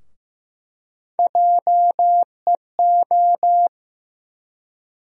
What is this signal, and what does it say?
Morse code, heard as a keyed tone, about 15 wpm, 700 Hz. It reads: JEO
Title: USA Call Sign Postfixes - 3 Letters 15wpm